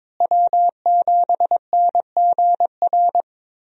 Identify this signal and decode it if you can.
Morse code, heard as a keyed tone, about 22 wpm, 700 Hz. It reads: W7NGR
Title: USA Call Signs - 5 Characters - Repeated 2x - 22wpm